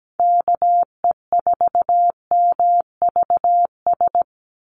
Morse code, heard as a keyed tone, about 17 wpm, 700 Hz. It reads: KE4MVS
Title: USA Call Signs 17wpm